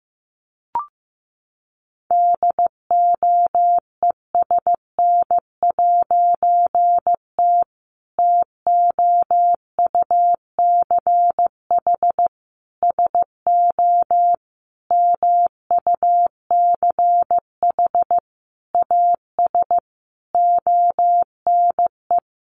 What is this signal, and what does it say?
Morse code, heard as a keyed tone, about 14 wpm, 700 Hz. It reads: DOESN'T TOUCH SO MUCH AS ONE
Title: A Sound Of Thunder 15wpm